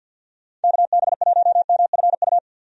Morse code, heard as a keyed tone, about 51 wpm, 700 Hz. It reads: KB1MFU
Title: USA Call Signs - 6 Characters - 50wpm